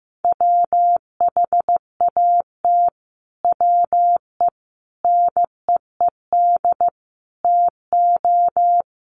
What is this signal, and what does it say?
Morse code, heard as a keyed tone, about 15 wpm, 700 Hz. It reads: WHAT WE NEED TO
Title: Sets of 5 Words Encore 15wpm